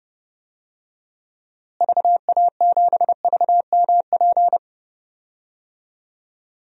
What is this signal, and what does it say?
Morse code, heard as a keyed tone, about 30 wpm, 700 Hz. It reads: VA7VMP
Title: DXCC-100 Call Signs - 6 Characters - Repeated 2x - 30wpm